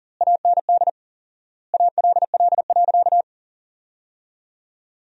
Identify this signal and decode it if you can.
Morse code, heard as a keyed tone, about 40 wpm, 700 Hz. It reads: AND ALL.
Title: 4x Word-Spacing - Sentences from Top 200 Words - 40wpm